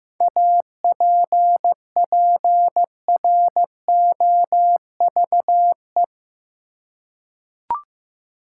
Morse code, heard as a keyed tone, about 15 wpm, 700 Hz. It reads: APPROVE
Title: Seven Letter Words - 15wpm